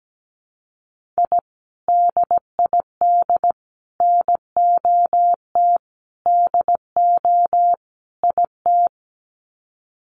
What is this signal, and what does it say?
Morse code, heard as a keyed tone, about 17 wpm, 700 Hz. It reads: I DID NOT DO IT
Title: Sets of 5 Words Encore2 17wpm